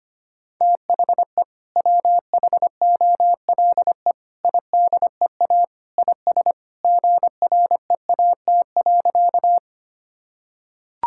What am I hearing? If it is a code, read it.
Morse code, heard as a keyed tone, about 25 wpm, 690 Hz. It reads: THE WHOLE IDEA IS GREAT.
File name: practice / 25wpm / Sentences with 2-Word Phrases - 25wpm